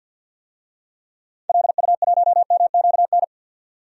Morse code, heard as a keyed tone, about 52 wpm, 700 Hz. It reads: LU1MQN
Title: DXCC-100 Call Signs - 6 Characters - 50wpm